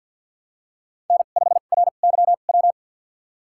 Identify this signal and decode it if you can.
Morse code, heard as a keyed tone, about 49 wpm, spaced as for 36 wpm, 700 Hz. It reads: N5RYW
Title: USA Call Signs - 5 Characters - 2x Character Spacing - 50wpm